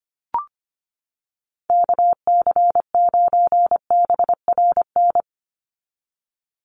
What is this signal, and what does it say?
Morse code, heard as a keyed tone, about 25 wpm, 700 Hz. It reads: KC9BRN
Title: USA Call Signs 25wpm